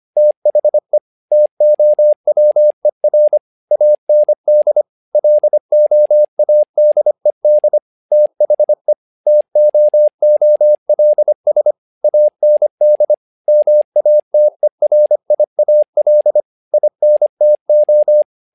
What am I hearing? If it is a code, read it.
Morse code, heard as a keyed tone, about 25 wpm, 600 Hz. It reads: THE TOWER AND LOADED THE TOOLS AND MATERIAL INTO
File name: learning center / tower trouble / tower trouble 25wpm